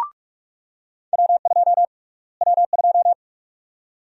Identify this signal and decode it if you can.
Morse code, heard as a keyed tone, about 42 wpm, 700 Hz. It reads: W2 W2
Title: USA Call Sign Prefixes - Repeated 2x - 45wpm